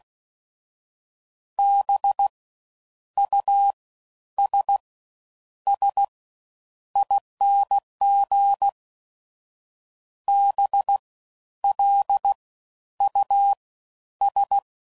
Morse code, heard as a keyed tone, about 16 wpm, 800 Hz. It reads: B U S S ING B L U S
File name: Suffixes / Special INGs / Fast Ings